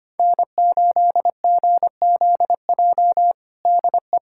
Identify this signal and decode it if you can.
Morse code, heard as a keyed tone, about 25 wpm, 700 Hz. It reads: N8GZJ DE
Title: K1USN SST QSO Practice 25wpm